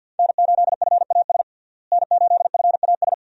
Morse code, heard as a keyed tone, about 53 wpm, 700 Hz. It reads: N8FAS N8FAS
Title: USA Call Signs - 5 Characters - Repeated 2x - 50wpm